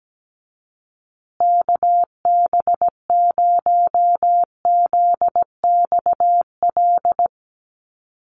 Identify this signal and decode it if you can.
Morse code, heard as a keyed tone, about 17 wpm, 700 Hz. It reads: KB0ZXL
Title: USA Call Signs Repeated 2x 17wpm